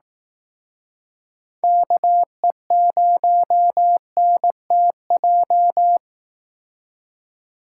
Morse code, heard as a keyed tone, about 18 wpm, 700 Hz. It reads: KE0NTJ
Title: Speed Racing - USA 2x3 Call Signs 15wpm